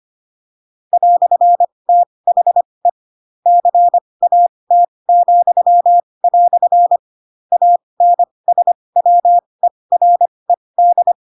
Morse code, heard as a keyed tone, about 25 wpm, spaced as for 20 wpm, 700 Hz. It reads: "THE CAT," ANSWERED
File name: Pinocchio / Pinocchio ch7 15 25 ik1hge